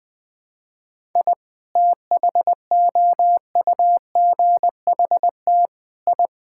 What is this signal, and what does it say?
Morse code, heard as a keyed tone, about 21 wpm, 700 Hz. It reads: I THOUGHT I
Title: Sets of 6 Words - 20wpm